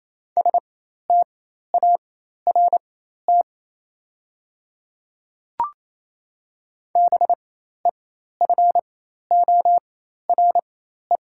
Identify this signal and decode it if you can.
Morse code, heard as a keyed tone, about 28 wpm, spaced as for 13 wpm, 690 Hz. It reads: START BEFORE
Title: 4x Character Spacing - Top 300 Words - 28wpm